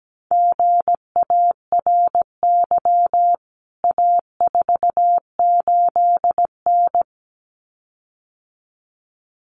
Morse code, heard as a keyed tone, about 17 wpm, 700 Hz. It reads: GARY A48N
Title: CWT Exchange 17wpm